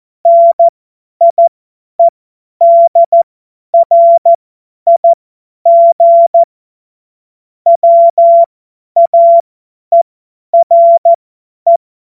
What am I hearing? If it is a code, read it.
Morse code, heard as a keyed tone, about 14 wpm, spaced as for 10 wpm, 680 Hz. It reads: NIEDRIG WAERE
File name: Morsekurs / Witze / Witz 13. Juli 2021, 50 F70